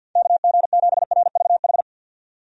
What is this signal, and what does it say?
Morse code, heard as a keyed tone, about 50 wpm, 700 Hz. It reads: KG7RVH
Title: USA Call Signs - 6 Characters - 50wpm